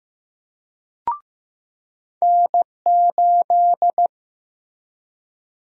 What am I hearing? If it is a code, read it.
Morse code, heard as a keyed tone, about 15 wpm, 700 Hz. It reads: N8
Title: USA Call Sign Prefixes - Repeated 2x - 15wpm